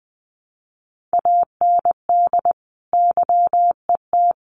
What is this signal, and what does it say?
Morse code, heard as a keyed tone, about 20 wpm, 700 Hz. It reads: AND YET
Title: Sentences from Top 1400 Words 20wpm